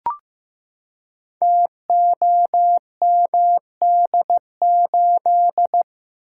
Morse code, heard as a keyed tone, about 15 wpm, 700 Hz. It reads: TOMD8
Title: Code Groups - 5 Letter:Numbers - 15wpm